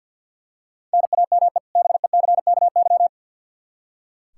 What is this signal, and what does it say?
Morse code, heard as a keyed tone, about 46 wpm, 700 Hz. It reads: NAME BECKY
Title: Practice QSO - Exchange 1 - Beginning - 50wpm